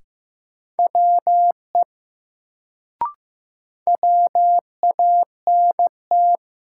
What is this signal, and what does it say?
Morse code, heard as a keyed tone, about 15 wpm, 700 Hz. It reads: WE WANT
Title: Sets of 2 Words - New Word Drill - 15wpm